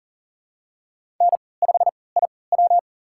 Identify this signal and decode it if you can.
Morse code, heard as a keyed tone, about 38 wpm, spaced as for 22 wpm, 700 Hz. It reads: N5IW
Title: USA Call Signs - 4 Characters - 3x Character Spacing - 40wpm